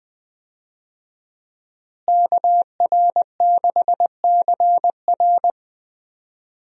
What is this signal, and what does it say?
Morse code, heard as a keyed tone, about 20 wpm, 700 Hz. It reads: KR6CR
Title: USA Call Signs - 5 Characters - Repeated 3x - 20wpm